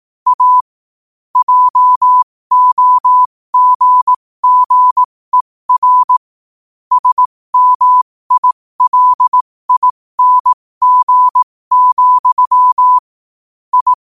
Morse code, heard as a keyed tone, about 18 wpm, spaced as for 15 wpm, 1000 Hz. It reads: A JOGGER SMILING, I
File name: qotdfiles / 15wpm